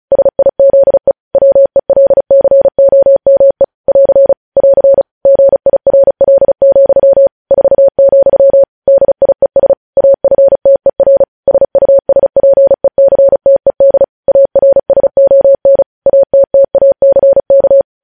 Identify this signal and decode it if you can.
Morse code, heard as a keyed tone, about 36 wpm, 550 Hz. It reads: SIZE WELCOME <AR> <AR> GIRL, 4, DIES AFTER SUSPECTED ARSON ATTACK
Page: News Headlines in Morse Code at 35 WPM